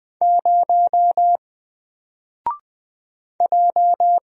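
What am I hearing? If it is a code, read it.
Morse code, heard as a keyed tone, about 20 wpm, 700 Hz. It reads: 0 J
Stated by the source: Single letter and number Morse drill at 20 wpm, from N4LCD.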